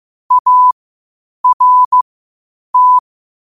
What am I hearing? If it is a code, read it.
Morse code, heard as a keyed tone, about 15 wpm, spaced as for 8 wpm, 1000 Hz. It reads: ART
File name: qotdfiles / 8wpm